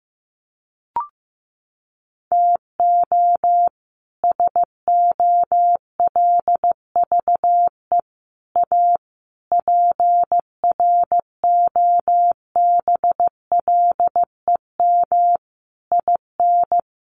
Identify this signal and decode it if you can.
Morse code, heard as a keyed tone, about 15 wpm, 700 Hz. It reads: TO SOLVE A PROBLEM IN
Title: Sentences from Top 1400 Words 15wpm